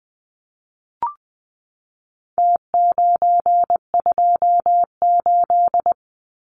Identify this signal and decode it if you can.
Morse code, heard as a keyed tone, about 20 wpm, 700 Hz. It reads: T928
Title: Numbers - Contest Style - 4-Digit 20wpm